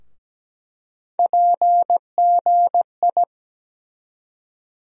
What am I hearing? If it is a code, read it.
Morse code, heard as a keyed tone, about 17 wpm, 700 Hz. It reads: PGI
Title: USA Call Sign Postfixes - 3 Letters 17wpm